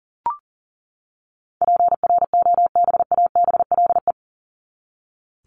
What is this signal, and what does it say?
Morse code, heard as a keyed tone, about 40 wpm, 700 Hz. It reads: PROBABLE
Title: LY Suffix - Top 100 Words with Base Words 40wpm